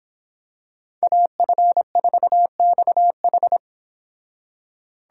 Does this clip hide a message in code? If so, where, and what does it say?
Yes: Morse code from 1.0 to 3.6 s, heard as a keyed tone, about 26 wpm, 700 Hz. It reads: AF4XH